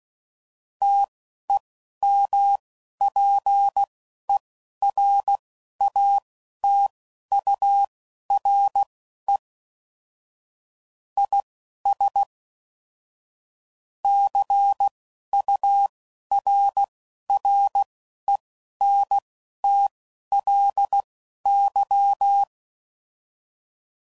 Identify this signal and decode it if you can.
Morse code, heard as a keyed tone, about 16 wpm, spaced as for 11 wpm, 800 Hz. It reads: TEMPERATURE IS CURRENTLY
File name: CW Code Groups @10WPM / Weather Sets / Weather 5 22